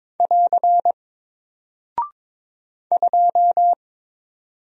Morse code, heard as a keyed tone, about 22 wpm, 700 Hz. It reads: <AR> 2
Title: Single Character (Letter:Number:Punctuation:Prosign) Alt 22wpm